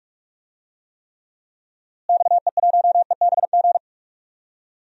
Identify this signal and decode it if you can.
Morse code, heard as a keyed tone, about 43 wpm, 700 Hz. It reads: XE1EBG
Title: DXCC-100 Call Signs - 6 Characters - Repeated 2x - 45wpm